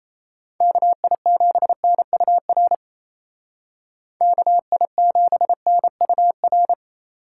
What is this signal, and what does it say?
Morse code, heard as a keyed tone, about 34 wpm, 700 Hz. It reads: KI7NUR KI7NUR
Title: Speed Racing - USA 2x3 Call Signs 28wpm